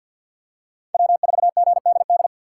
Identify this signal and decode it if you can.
Morse code, heard as a keyed tone, about 47 wpm, 700 Hz. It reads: W4GDD
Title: USA Call Signs - 5 Characters - 6x Character Spacing - 50wpm